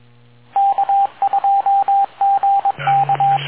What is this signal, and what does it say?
Morse code, heard as a keyed tone, about 22 wpm, 790 Hz, with radio noise. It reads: K2GK